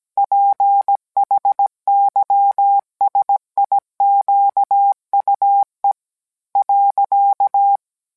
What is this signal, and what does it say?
Morse code, heard as a keyed tone, about 17 wpm, 800 Hz. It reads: PHYSIQUE .